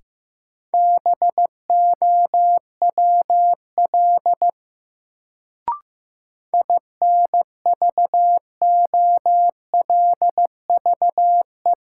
Morse code, heard as a keyed tone, about 15 wpm, 700 Hz. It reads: BOWL INVOLVE
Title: Sentences from Top 1200 Words - New Word Drill - 15wpm